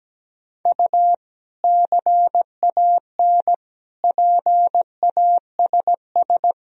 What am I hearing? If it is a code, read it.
Morse code, heard as a keyed tone, about 17 wpm, 700 Hz. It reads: U CAN PASS
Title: Sentences from Top 400 Words and Abbreviations 17wpm